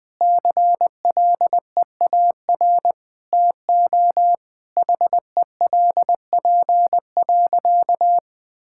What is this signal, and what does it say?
Morse code, heard as a keyed tone, about 20 wpm, 700 Hz. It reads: CLEAR TO HELP.
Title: Sentences with 8-Word Phrases - 20wpm